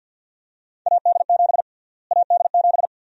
Morse code, heard as a keyed tone, about 52 wpm, 700 Hz. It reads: AD7 AD7
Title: USA Call Sign Prefixes - Repeated 2x - 50wpm